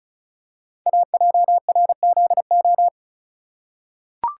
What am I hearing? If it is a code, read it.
Morse code, heard as a keyed tone, about 36 wpm, 700 Hz. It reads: AJRZO
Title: Code Groups - 5 Letter:Numbers - 35wpm